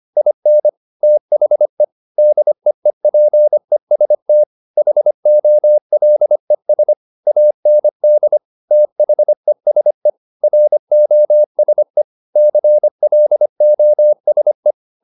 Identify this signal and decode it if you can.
Morse code, heard as a keyed tone, about 25 wpm, 600 Hz. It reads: IN THE DEEPEST HOLES AND THESE ROSE CLOSE
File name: the old man and the sea part 1 25wpm